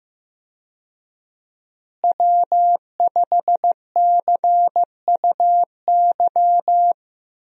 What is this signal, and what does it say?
Morse code, heard as a keyed tone, about 16 wpm, 700 Hz. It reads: W5CUY
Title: USA Call Signs - 5 Characters - Repeated 2x - 15wpm